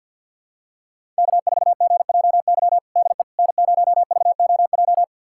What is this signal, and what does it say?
Morse code, heard as a keyed tone, about 53 wpm, 700 Hz. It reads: K4GJY DE N0VOJ
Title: USA Callsign DE Callsign - 5 Characters - 50wpm